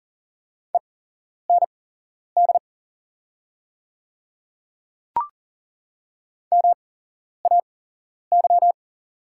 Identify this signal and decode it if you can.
Morse code, heard as a keyed tone, about 43 wpm, spaced as for 11 wpm, 690 Hz. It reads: END MAY
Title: 8x Character Spacing - Top 200 Words - 40wpm